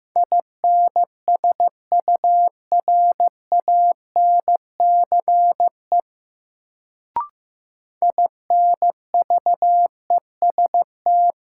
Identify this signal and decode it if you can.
Morse code, heard as a keyed tone, about 15 wpm, 700 Hz. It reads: INSURANCE INVEST
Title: Sentences from Top 1100 Words - New Word Drill - 15wpm